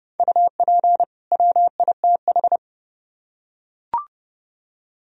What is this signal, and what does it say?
Morse code, heard as a keyed tone, about 30 wpm, 700 Hz. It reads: UP WITH